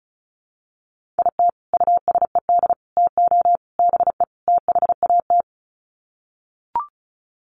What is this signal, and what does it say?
Morse code, heard as a keyed tone, about 34 wpm, 700 Hz. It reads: IT USED TO BE THAT
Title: Sets of 5 Words Encore2 35wpm